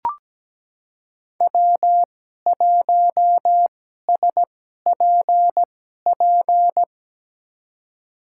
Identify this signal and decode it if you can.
Morse code, heard as a keyed tone, about 17 wpm, spaced as for 12 wpm, 700 Hz. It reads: W1SPP
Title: USA Call Signs - 2x Character Spacing - 17wpm